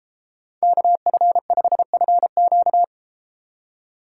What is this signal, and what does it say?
Morse code, heard as a keyed tone, about 32 wpm, 700 Hz. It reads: KF5FQ